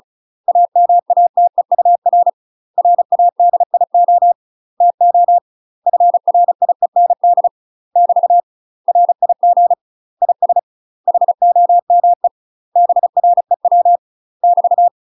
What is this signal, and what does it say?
Morse code, heard as a keyed tone, about 36 wpm, 700 Hz. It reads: AMATEUR RADIO TO FRIEND <BT> RIG IS HOME BREW <BT>
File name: UK MASTER QSOs TRAINING / 35WPM / UK MASTER QSOs 0002